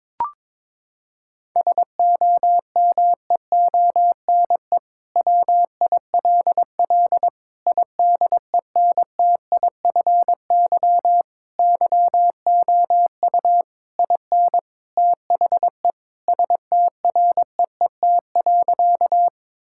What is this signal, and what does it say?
Morse code, heard as a keyed tone, about 22 wpm, 700 Hz. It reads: SOMEONE WILL IDENTIFY YOU IN THE STREET.
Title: Sentences from Top 1000 Words - Review Part 6 - 22wpm